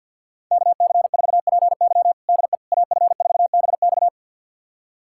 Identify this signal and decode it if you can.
Morse code, heard as a keyed tone, about 53 wpm, 700 Hz. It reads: KC4PY DE AF4BX
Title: USA Callsign DE Callsign - 5 Characters - 50wpm